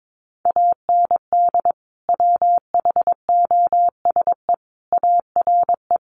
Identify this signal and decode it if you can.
Morse code, heard as a keyed tone, about 22 wpm, 700 Hz. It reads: AND WHOSE ARE
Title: Sentences from Top 1200 Words 22wpm